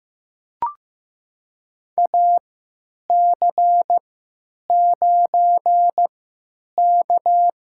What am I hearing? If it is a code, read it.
Morse code, heard as a keyed tone, about 15 wpm, spaced as for 9 wpm, 700 Hz. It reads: AC9K
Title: USA Call Signs - 4 Characters - 3x Character Spacing - 15wpm